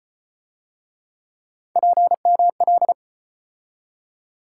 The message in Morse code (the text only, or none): PML